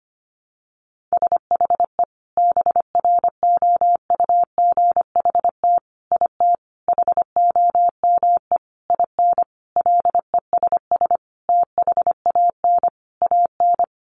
Morse code, heard as a keyed tone, about 25 wpm, 700 Hz. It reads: SHE BROUGHT IT HOME IN LESS THAN AN
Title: Sentences from Top 500 Words 25wpm